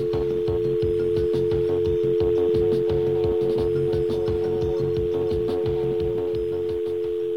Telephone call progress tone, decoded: dial tone